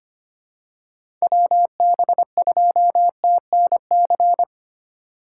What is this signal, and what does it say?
Morse code, heard as a keyed tone, about 25 wpm, 700 Hz. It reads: WB2TNC
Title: USA Call Signs - 4x Character Spacing - 25wpm